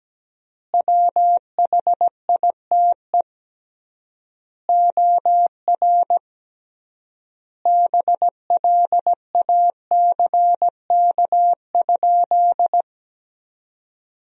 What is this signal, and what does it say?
Morse code, heard as a keyed tone, about 17 wpm, 700 Hz. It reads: WHITE OR BLACK?
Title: 3x Word-Spacing - Sentences from Top 400 Words - 17wpm